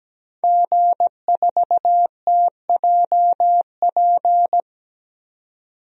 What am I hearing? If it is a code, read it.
Morse code, heard as a keyed tone, about 17 wpm, 700 Hz. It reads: G4TJP
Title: DXCC-100 Call Signs - 5 Characters - 17wpm